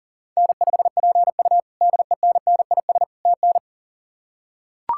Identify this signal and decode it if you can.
Morse code, heard as a keyed tone, about 40 wpm, 700 Hz. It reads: N5PU DENNIS TN
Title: K1USN SST QSO Practice - 40wpm